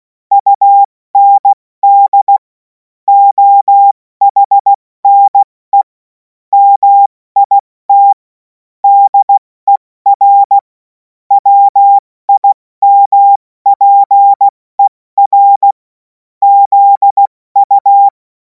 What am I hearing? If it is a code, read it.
Morse code, heard as a keyed tone, about 16 wpm, spaced as for 14 wpm, 800 Hz. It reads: UND OHNE MIT DER WIMPER ZU
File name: Morsekurs / Witze / Witz 24. Februar 2026 70 F80